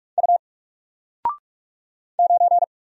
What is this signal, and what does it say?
Morse code, heard as a keyed tone, about 44 wpm, 700 Hz. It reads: U 9
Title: Single Letter-Number Alt 45wpm